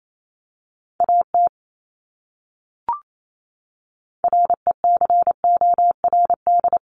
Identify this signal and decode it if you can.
Morse code, heard as a keyed tone, about 28 wpm, 700 Hz. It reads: AT RECORD
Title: Top 500 Words Repeated 28wpm